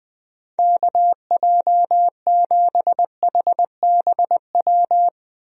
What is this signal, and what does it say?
Morse code, heard as a keyed tone, about 20 wpm, 700 Hz. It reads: KJ7HBW
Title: USA Call Signs - 6 Characters - Repeated 3x - 20wpm